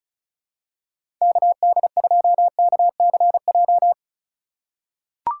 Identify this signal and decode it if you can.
Morse code, heard as a keyed tone, about 34 wpm, 700 Hz. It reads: KD2KCJ
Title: USA Call Signs - 6 Characters - 4x Character Spacing - 35wpm